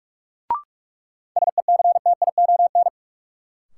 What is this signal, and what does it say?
Morse code, heard as a keyed tone, about 41 wpm, 700 Hz. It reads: SECTION
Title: Sentences from Top 1000 Words - Review Part 5 - New Word Drill - 45wpm